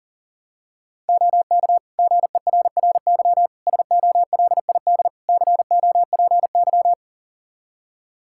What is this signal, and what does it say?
Morse code, heard as a keyed tone, about 40 wpm, 700 Hz. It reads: OK GERRY SOLID COPY
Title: Practice QSO - Exchange 3 - More Info - 40wpm